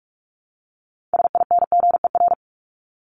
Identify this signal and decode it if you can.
Morse code, heard as a keyed tone, about 44 wpm, 700 Hz. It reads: SINGER
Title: Top 1501-1750 Words 45wpm